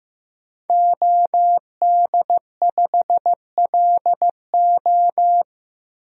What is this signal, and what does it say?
Morse code, heard as a keyed tone, about 15 wpm, 700 Hz. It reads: OD5LO
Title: DXCC-100 Call Signs - 5 Characters - Repeated 3x - 15wpm